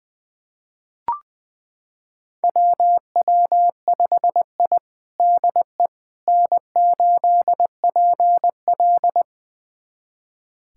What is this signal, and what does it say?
Morse code, heard as a keyed tone, about 20 wpm, 700 Hz. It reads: WW5I DE N8PL
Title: USA Callsign DE Callsign - 4 Characters - 20wpm